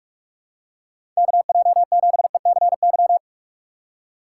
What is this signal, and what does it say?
Morse code, heard as a keyed tone, about 46 wpm, 700 Hz. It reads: KJ7ECY
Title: USA Call Signs - 6 Characters - 4x Character Spacing - 45wpm